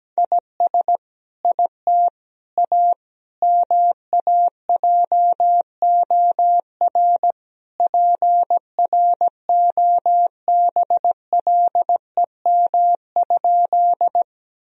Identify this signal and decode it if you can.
Morse code, heard as a keyed tone, about 17 wpm, 700 Hz. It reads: IS IT A MAJOR PROBLEM?
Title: Sentences from Top 1000 Words - Review Part 4 - 17wpm